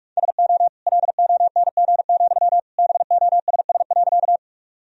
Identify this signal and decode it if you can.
Morse code, heard as a keyed tone, about 46 wpm, 700 Hz. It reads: SO LONG, BOSS.
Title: Sentences from Top 1000 Words - Review Part 4 - 45wpm